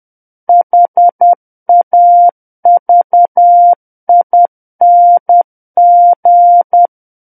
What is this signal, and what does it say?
Morse code, heard as a keyed tone, about 10 wpm, 700 Hz. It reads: HAVING
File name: Book One / 01-Chapter 1 - 10 WPM